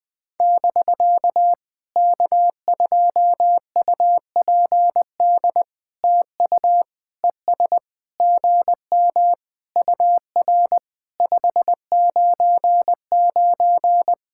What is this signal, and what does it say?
Morse code, heard as a keyed tone, about 20 wpm, 700 Hz. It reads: <BK> K2UPD TU ES GM UR 599
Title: POTA QSO Practice - 20wpm